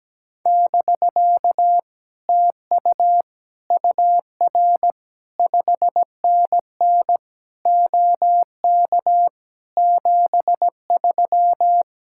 Morse code, heard as a keyed tone, about 17 wpm, 700 Hz. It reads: <BK> TU UR 5NN OK 73
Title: Speed-Racing Common QSO Phrases - 17wpm